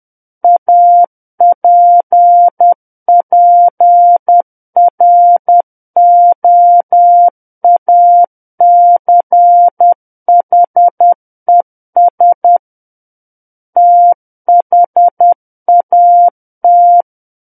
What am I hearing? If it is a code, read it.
Morse code, heard as a keyed tone, about 10 wpm, 700 Hz. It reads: APPROACHES THAT
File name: Book One / 01-Chapter 1 - 10 WPM